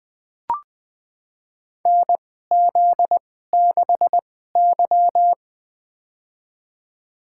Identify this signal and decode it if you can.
Morse code, heard as a keyed tone, about 20 wpm, spaced as for 14 wpm, 700 Hz. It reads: NZ6Y
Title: USA Call Signs - 4 Characters - 2x Character Spacing - 20wpm